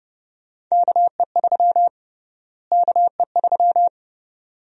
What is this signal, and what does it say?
Morse code, heard as a keyed tone, about 30 wpm, 700 Hz. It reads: KE3 KE3
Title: USA Call Sign Prefixes - Repeated 2x - 30wpm